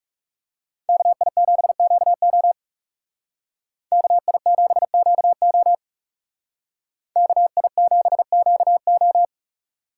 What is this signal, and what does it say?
Morse code, heard as a keyed tone, about 36 wpm, 700 Hz. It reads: KI7QO KI7QO KI7QO
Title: Speed Racing - USA 2x2 Call Signs 30wpm